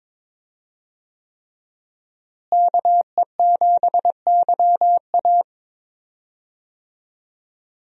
Morse code, heard as a keyed tone, about 22 wpm, 700 Hz. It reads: KE7YA
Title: USA Call Signs - 5 Characters - Repeated 3x - 22wpm